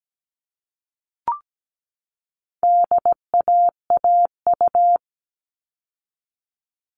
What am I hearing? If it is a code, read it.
Morse code, heard as a keyed tone, about 17 wpm, 700 Hz. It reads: DAAU
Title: All Cut Numbers - 4-Digit 17wpm